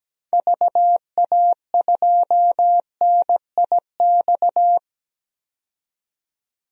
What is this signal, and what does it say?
Morse code, heard as a keyed tone, about 17 wpm, 700 Hz. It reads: VA2NIX
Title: DXCC-100 Call Signs - 6 Characters - 17wpm